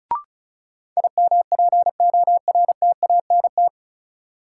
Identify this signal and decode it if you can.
Morse code, heard as a keyed tone, about 36 wpm, 700 Hz. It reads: IMPORTANT
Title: Sets of 3 Words - New Word Drill - 35wpm